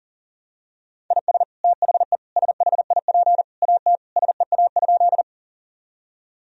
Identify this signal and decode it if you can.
Morse code, heard as a keyed tone, about 40 wpm, 700 Hz. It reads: IS THE SHIP AT SEA?
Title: Sentences from Top 1000 Words - Review Part 2 - 40wpm